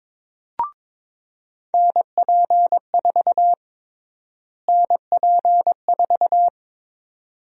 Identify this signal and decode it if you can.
Morse code, heard as a keyed tone, about 22 wpm, 700 Hz. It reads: NP4 NP4
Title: USA Call Sign Prefixes - Repeated 2x - 22wpm